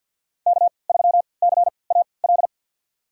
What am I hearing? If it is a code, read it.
Morse code, heard as a keyed tone, about 47 wpm, spaced as for 28 wpm, 690 Hz. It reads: K3CAL
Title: USA Call Signs - 5 Characters - 3x Character Spacing - 50wpm